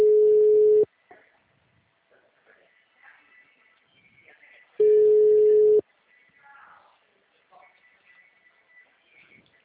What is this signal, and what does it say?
Ringback tone (Europe, 425 Hz).